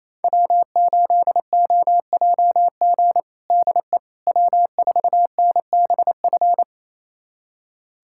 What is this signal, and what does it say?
Morse code, heard as a keyed tone, about 28 wpm, 700 Hz. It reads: W8OJG DE W4NBF
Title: USA Callsign DE Callsign - 5 Characters - 28wpm